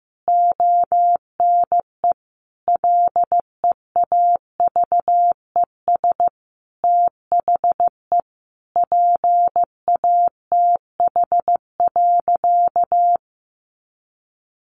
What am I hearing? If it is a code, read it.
Morse code, heard as a keyed tone, about 15 wpm, 700 Hz. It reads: ONE LEAVES THE PATH.
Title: A Sound Of Thunder 15wpm